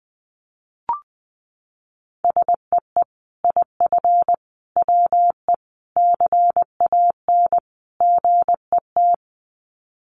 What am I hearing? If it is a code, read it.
Morse code, heard as a keyed tone, about 19 wpm, 700 Hz. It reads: SEE IF WE CAN GET
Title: Sets of 5 Words Encore2 20wpm